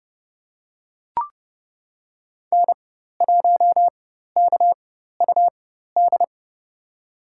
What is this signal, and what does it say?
Morse code, heard as a keyed tone, about 30 wpm, spaced as for 14 wpm, 700 Hz. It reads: N1KUD
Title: USA Call Signs - 5 Characters - 4x Character Spacing - 30wpm